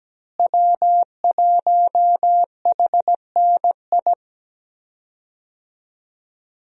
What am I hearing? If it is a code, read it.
Morse code, heard as a keyed tone, about 17 wpm, 700 Hz. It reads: W1HNI
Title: USA Call Signs - 5 Characters - Repeated 2x - 17wpm